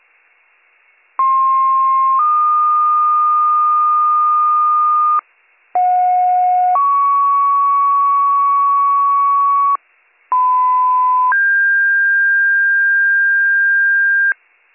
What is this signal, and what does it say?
Two-tone page, heard as a one-second tone, then a three-second tone, about 1060 then 1220 Hz, about 730 then 1120 Hz, about 1010 then 1600 Hz. It is heard from 1.2 to 14.3 s.